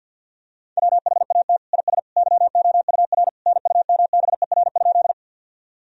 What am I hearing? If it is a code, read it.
Morse code, heard as a keyed tone, about 51 wpm, 700 Hz. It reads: WHAT IS YOUR NUMBER?
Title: Sentences from Top 1000 Words - Review Part 1 - 50wpm